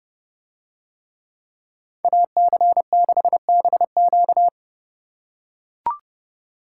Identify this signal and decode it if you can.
Morse code, heard as a keyed tone, about 28 wpm, 700 Hz. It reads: AC6BQ